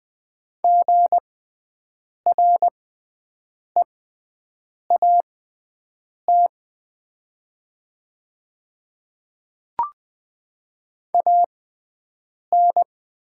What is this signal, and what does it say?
Morse code, heard as a keyed tone, about 20 wpm, spaced as for 7 wpm, 700 Hz. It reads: GREAT AN
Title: 6x Character Spacing - Top 300 Words - 20wpm